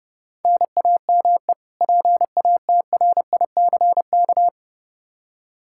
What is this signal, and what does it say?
Morse code, heard as a keyed tone, about 32 wpm, 700 Hz. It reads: NAME PATRICK
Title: Practice QSO - Exchange 1 - Beginning - 30wpm